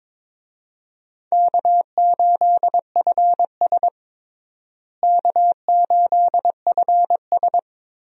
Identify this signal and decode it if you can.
Morse code, heard as a keyed tone, about 22 wpm, 700 Hz. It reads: K8FS K8FS
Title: USA Call Signs - 4 Characters - Repeated 3x - 22wpm